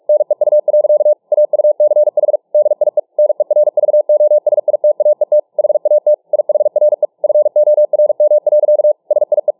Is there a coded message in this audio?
Yes: Morse code from 0.1 to 9.5 s, heard as a keyed tone, about 46 wpm, 600 Hz. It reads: NEU. AUCH DIE NERVOSITAET HAT IHRE FORM. SIE